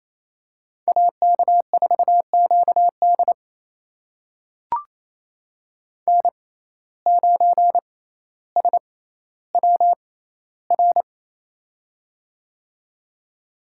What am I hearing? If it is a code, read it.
Morse code, heard as a keyed tone, about 28 wpm, spaced as for 14 wpm, 700 Hz. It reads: AK4QD N9SWR
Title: USA Call Signs - 5 Characters - 6x Character Spacing - 28wpm